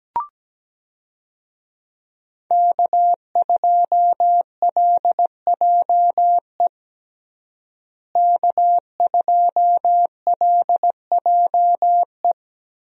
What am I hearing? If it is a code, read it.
Morse code, heard as a keyed tone, about 17 wpm, 700 Hz. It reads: K2LJE K2LJE
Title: USA Call Signs - 5 Characters - Repeated 3x - 17wpm